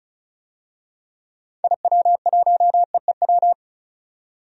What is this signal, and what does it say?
Morse code, heard as a keyed tone, about 36 wpm, 700 Hz. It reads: IW1EEW
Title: DXCC-100 Call Signs - 6 Characters - 35wpm